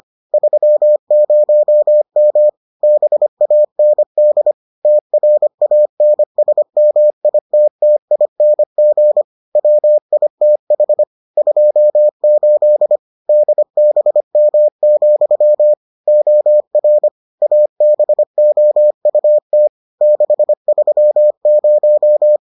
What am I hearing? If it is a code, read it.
Morse code, heard as a keyed tone, about 25 wpm, 600 Hz. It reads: 30M BAND TRANSMITTING WITH 28 DBM, OR ABOUT 630